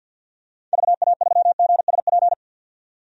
Morse code, heard as a keyed tone, about 53 wpm, 700 Hz. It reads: VA3GSP
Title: DXCC-100 Call Signs - 6 Characters - 50wpm